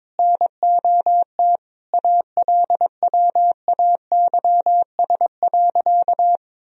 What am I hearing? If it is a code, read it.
Morse code, heard as a keyed tone, about 22 wpm, 700 Hz. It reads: NOT ALWAYS.
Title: Sentences from Top 1000 Words - Review Part 4 - 22wpm